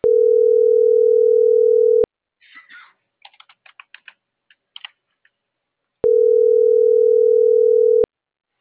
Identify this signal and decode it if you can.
Ringback tone (North America), 440 + 480 Hz.